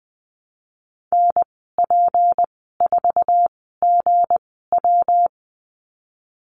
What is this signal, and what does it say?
Morse code, heard as a keyed tone, about 20 wpm, spaced as for 15 wpm, 700 Hz. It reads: NP4GW
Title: 173 Lesson 28g - Number 9 - Call Signs 20wpm